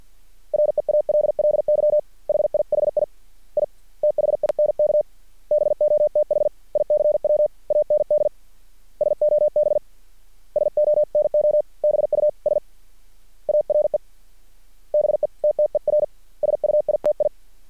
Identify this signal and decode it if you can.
Morse code, heard as a keyed tone, about 50 wpm, 600 Hz. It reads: REALLY HIHI I THINK BOTH ICW AND SOB SONO BUS ARE BETTER SUITI
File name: archived / CRSnet